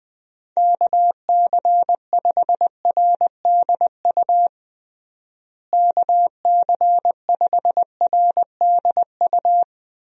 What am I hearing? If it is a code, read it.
Morse code, heard as a keyed tone, about 21 wpm, 700 Hz. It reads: KC5RDU KC5RDU
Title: USA Call Signs - 6 Characters - Repeated 3x - 20wpm